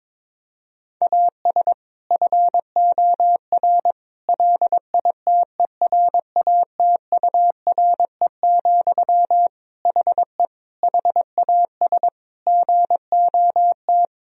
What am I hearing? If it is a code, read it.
Morse code, heard as a keyed tone, about 22 wpm, 700 Hz. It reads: AS FOR LITERATURE, HE HAS GOT
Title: Sentences from Top 1000 Words - Review Part 3 - 22wpm